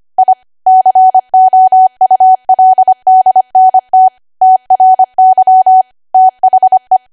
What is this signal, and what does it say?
Morse code, heard as a keyed tone, about 25 wpm, 750 Hz. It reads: I COULDNT TRY THE
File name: Archive / 25 WPM / 25WPM